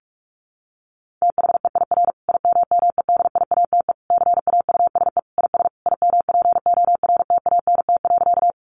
Morse code, heard as a keyed tone, about 44 wpm, 700 Hz. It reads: THEIR IMMEDIATE CAUSE IS IMPORTANT.
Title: Sentences from Top 1200 Words 45wpm